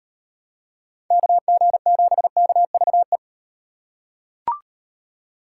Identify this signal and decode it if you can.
Morse code, heard as a keyed tone, about 38 wpm, 700 Hz. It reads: KG7KVE